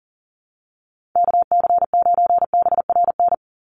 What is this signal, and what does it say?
Morse code, heard as a keyed tone, about 40 wpm, 700 Hz. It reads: KC9BRN
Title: USA Call Signs 40wpm